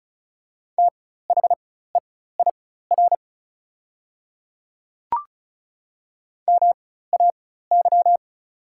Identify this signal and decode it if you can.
Morse code, heard as a keyed tone, about 36 wpm, spaced as for 16 wpm, 690 Hz. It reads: THEIR MAY
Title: 4x Character Spacing - Top 100 Words - 35wpm